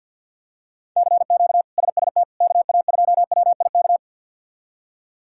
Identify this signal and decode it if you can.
Morse code, heard as a keyed tone, about 50 wpm, 700 Hz. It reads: CQ SST KA2WIK
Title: K1USN SST QSO Practice - 50wpm